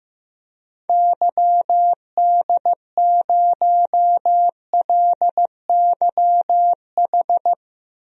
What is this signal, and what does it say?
Morse code, heard as a keyed tone, about 15 wpm, 700 Hz. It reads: YD0LYH
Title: DXCC-100 Call Signs - 6 Characters - Repeated 2x - 15wpm